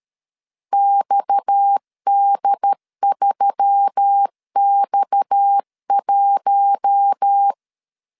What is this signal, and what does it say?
Morse code, heard as a keyed tone, about 13 wpm, 800 Hz. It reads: XD3X1
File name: LEZIONI / 6 PKX0 / 6 PKX 12wpm 2024